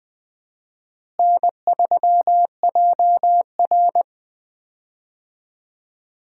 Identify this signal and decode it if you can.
Morse code, heard as a keyed tone, about 20 wpm, 700 Hz. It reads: N3JR
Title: USA Call Signs - 4 Characters - Repeated 3x - 20wpm